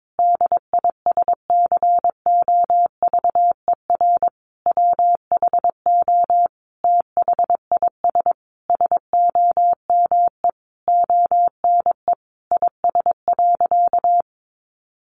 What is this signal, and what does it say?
Morse code, heard as a keyed tone, about 22 wpm, 700 Hz. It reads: DISCOVER WHO THIS SOME ONE IS.
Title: Sentences from Top 1100 Words 22wpm